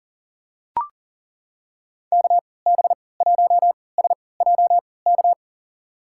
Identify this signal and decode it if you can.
Morse code, heard as a keyed tone, about 40 wpm, spaced as for 23 wpm, 700 Hz. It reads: KB1SJK
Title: USA Call Signs - 6 Characters - 3x Character Spacing - 40wpm